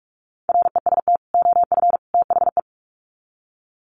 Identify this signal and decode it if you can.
Morse code, heard as a keyed tone, about 44 wpm, 700 Hz. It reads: REST OF THE
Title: Sets of 3 Words 45wpm